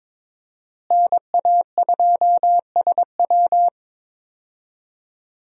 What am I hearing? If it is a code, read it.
Morse code, heard as a keyed tone, about 22 wpm, 700 Hz. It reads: NA2SW
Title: USA Call Signs - 5 Characters - Repeated 3x - 22wpm